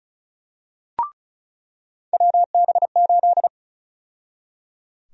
Morse code, heard as a keyed tone, about 36 wpm, 700 Hz. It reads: WB8
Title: USA Call Sign Prefixes - 35wpm